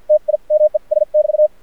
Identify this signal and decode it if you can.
Morse code, heard as a keyed tone, about 52 wpm, spaced as for 40 wpm, 610 Hz. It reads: TIMES <BT>
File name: archived / CRSnet